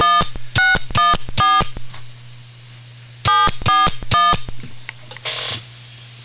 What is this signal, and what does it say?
DTMF, heard as telephone touch tones, tones of about 190 ms. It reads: 1614 741